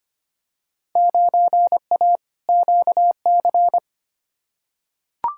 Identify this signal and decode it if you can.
Morse code, heard as a keyed tone, about 25 wpm, 700 Hz. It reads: 9A QC
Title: Field Day Exchanges Encore - 25wpm